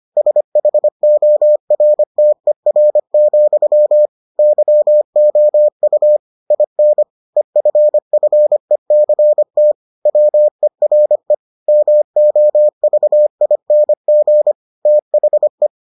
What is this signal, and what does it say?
Morse code, heard as a keyed tone, about 25 wpm, 600 Hz. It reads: SHORTER, YOU IN EFFECT WERE MOVING THE